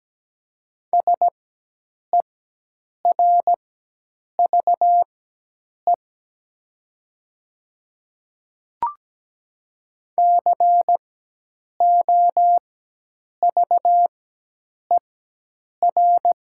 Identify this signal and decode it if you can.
Morse code, heard as a keyed tone, about 17 wpm, spaced as for 8 wpm, 700 Hz. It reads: SERVE COVER
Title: 4x Character Spacing - Top 500 Words - 17wpm